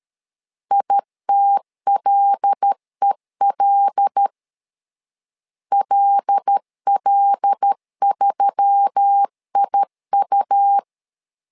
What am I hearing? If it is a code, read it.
Morse code, heard as a keyed tone, about 13 wpm, 800 Hz. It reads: ITLEL LL3IU
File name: LEZIONI / 5 LBVC / 5 L 12wpm 1913